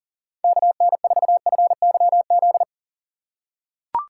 Morse code, heard as a keyed tone, about 37 wpm, 700 Hz. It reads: KN4FYZ
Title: USA Call Signs - 6 Characters - 2x Character Spacing - 40wpm